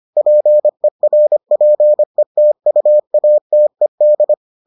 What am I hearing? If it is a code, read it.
Morse code, heard as a keyed tone, about 25 wpm, 600 Hz. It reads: PERPETUATED